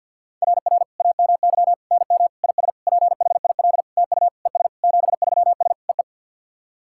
Morse code, heard as a keyed tone, about 50 wpm, 700 Hz. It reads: RR AMY NM IS PHIL TU ES 73S EE
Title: SOTA QSO Practice 50wpm